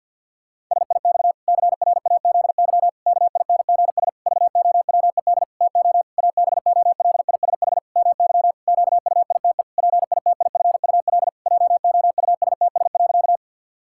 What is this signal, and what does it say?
Morse code, heard as a keyed tone, about 50 wpm, 700 Hz. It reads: SIX CRAZY KINGS VOWED TO ABOLISH MY QUITE PITIFUL JOUSTS.
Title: Pangram Challenge v2 - 50wpm